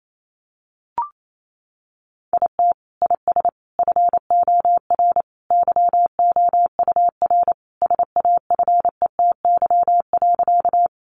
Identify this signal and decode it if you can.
Morse code, heard as a keyed tone, about 28 wpm, 700 Hz. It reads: IT IS FOR YOUR SAFETY.
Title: Sentences from Top 700 Words 28wpm